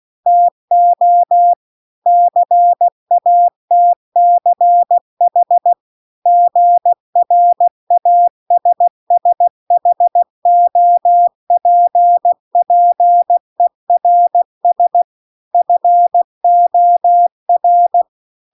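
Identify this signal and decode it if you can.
Morse code, heard as a keyed tone, about 16 wpm, 700 Hz. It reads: TO CATCH GRASSHOPPERS FOR
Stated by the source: Morse practice text at 16 WPM.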